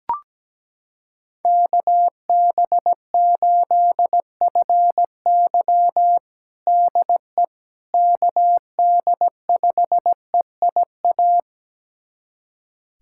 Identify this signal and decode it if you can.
Morse code, heard as a keyed tone, about 17 wpm, 700 Hz. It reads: KB8FY DE KD5EIA
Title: USA Callsign DE Callsign 17wpm